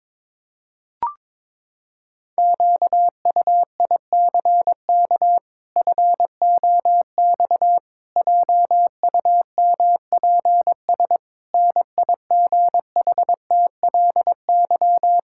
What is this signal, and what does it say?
Morse code, heard as a keyed tone, about 21 wpm, 700 Hz. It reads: QUICK FOX JUMPS NIGHTLY
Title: Pangram Challenge v2 - 22wpm